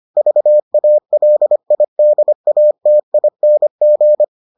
Morse code, heard as a keyed tone, about 25 wpm, 600 Hz. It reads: VALIDATING